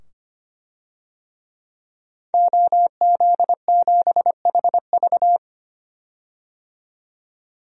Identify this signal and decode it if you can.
Morse code, heard as a keyed tone, about 25 wpm, 700 Hz. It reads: OZ7HV
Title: DXCC-100 Call Signs - 5 Characters - 25wpm